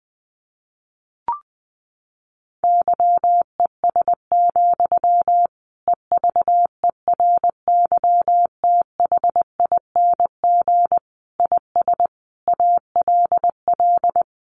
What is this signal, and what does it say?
Morse code, heard as a keyed tone, about 20 wpm, 700 Hz. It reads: YES, EVERYTHING IS ALL
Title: Sentences from Top 1100 Words 20wpm